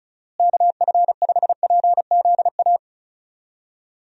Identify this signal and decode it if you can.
Morse code, heard as a keyed tone, about 34 wpm, 700 Hz. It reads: KF5PZA
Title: USA Call Signs - 6 Characters - Repeated 3x - 35wpm